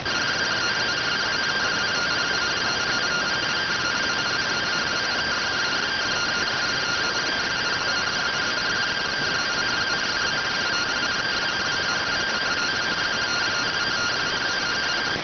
Two-tone teleprinter signal, about 170 Hz shift.